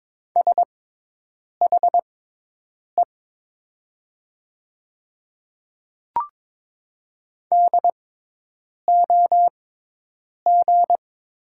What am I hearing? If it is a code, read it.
Morse code, heard as a keyed tone, about 22 wpm, spaced as for 8 wpm, 700 Hz. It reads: SHE DOG
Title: 6x Character Spacing - Top 500 Words - 22wpm